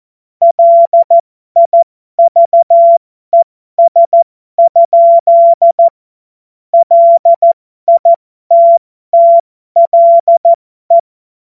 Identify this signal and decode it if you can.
Morse code, heard as a keyed tone, about 14 wpm, spaced as for 12 wpm, 680 Hz. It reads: LIVES? LITTLE
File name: Morsekurs / Witze